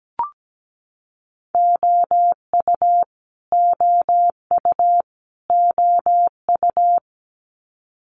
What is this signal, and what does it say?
Morse code, heard as a keyed tone, about 17 wpm, 700 Hz. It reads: OU OU OU
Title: Multi-letter Phonemes Repeated 3x 17wpm